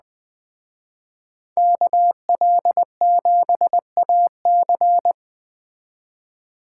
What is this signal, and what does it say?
Morse code, heard as a keyed tone, about 20 wpm, 700 Hz. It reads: KL7AC
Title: Speed Racing - USA 2x2 Call Signs 17wpm